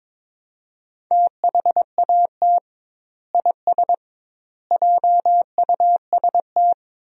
Morse code, heard as a keyed tone, about 21 wpm, 700 Hz. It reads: THAT IS JUST
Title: 2x Word-Spacing - Sentences from Top 200 Words - 22wpm